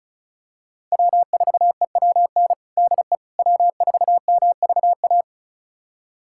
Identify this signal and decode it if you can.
Morse code, heard as a keyed tone, about 36 wpm, 700 Hz. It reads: W4EWN DE W4MVA
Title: USA Callsign DE Callsign - 5 Characters - 35wpm